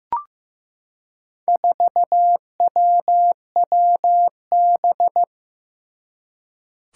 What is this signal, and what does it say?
Morse code, heard as a keyed tone, about 15 wpm, 700 Hz. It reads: 4WWB
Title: Four Challenging Letter-Number Combinations 15wpm